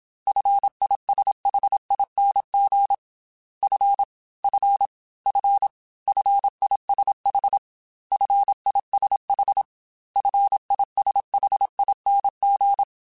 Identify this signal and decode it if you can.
Morse code, heard as a keyed tone, about 27 wpm, 800 Hz. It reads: FISHING F F F FISH FISH FISHING
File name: F and L / F and L mp3 / F and L Prefixes 3x2 Twice @25